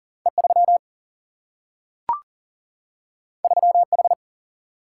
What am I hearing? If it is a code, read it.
Morse code, heard as a keyed tone, about 42 wpm, 700 Hz. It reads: E3 3H
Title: Single-Letter Single-Number 40wpm